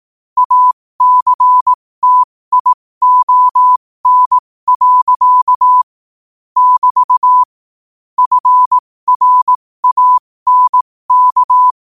Morse code, heard as a keyed tone, about 18 wpm, spaced as for 15 wpm, 1000 Hz. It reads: ACTION. <BT> FRANK
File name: qotdfiles / 15wpm